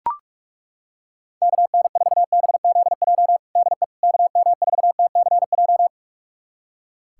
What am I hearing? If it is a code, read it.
Morse code, heard as a keyed tone, about 42 wpm, 700 Hz. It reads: KN4BZJ DE KM4TCJ
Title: USA Callsign DE Callsign - 6 Characters - 45wpm